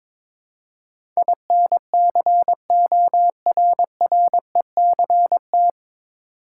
Morse code, heard as a keyed tone, about 22 wpm, 700 Hz. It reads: INCORRECT
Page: Morse Code Ninja practice 22wpm